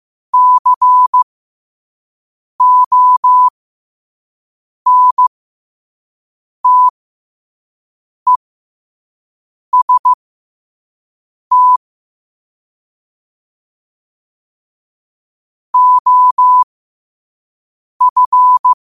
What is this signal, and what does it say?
Morse code, heard as a keyed tone, about 15 wpm, spaced as for 5 wpm, 1000 Hz. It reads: CONTEST OF
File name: qotdfiles / 5wpm